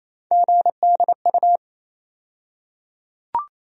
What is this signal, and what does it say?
Morse code, heard as a keyed tone, about 28 wpm, 700 Hz. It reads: GDU
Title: USA Call Sign Postfixes - 3 Letters 28wpm